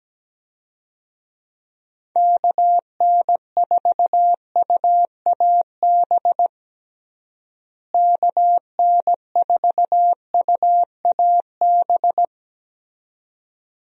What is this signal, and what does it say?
Morse code, heard as a keyed tone, about 17 wpm, 700 Hz. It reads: KN4UAB KN4UAB
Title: USA Call Signs - 6 Characters - Repeated 2x - 17wpm